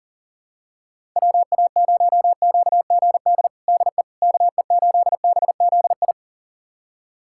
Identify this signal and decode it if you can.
Morse code, heard as a keyed tone, about 37 wpm, 700 Hz. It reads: WA0QGD DE KE8BZI
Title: USA Callsign DE Callsign 40wpm